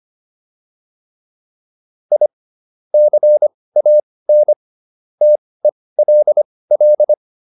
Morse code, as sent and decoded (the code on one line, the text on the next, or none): .. / -.-. .- -. / - . .-.. .-..
I CAN TELL